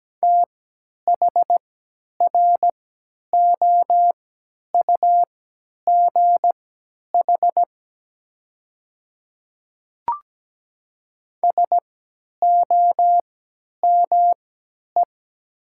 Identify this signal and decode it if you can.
Morse code, heard as a keyed tone, about 17 wpm, spaced as for 10 wpm, 700 Hz. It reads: THROUGH SOME
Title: 3x Character Spacing - Top 200 Words - 17wpm